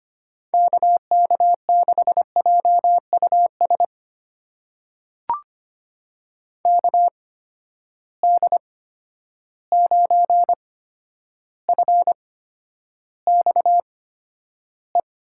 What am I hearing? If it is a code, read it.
Morse code, heard as a keyed tone, about 25 wpm, spaced as for 11 wpm, 700 Hz. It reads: KK6JUS KD9FXE
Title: USA Call Signs - 6 Characters - 8x Character Spacing - 25wpm